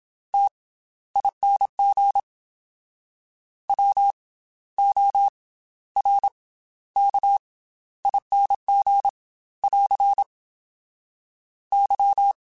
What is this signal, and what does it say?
Morse code, heard as a keyed tone, about 27 wpm, spaced as for 14 wpm, 800 Hz. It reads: TING WORKING <AR> Y
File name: Special INGs / Special INGs-mp3s / Slow ing @25